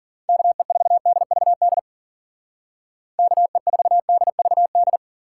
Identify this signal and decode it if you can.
Morse code, heard as a keyed tone, about 53 wpm, 700 Hz. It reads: KE4DVD KE4DVD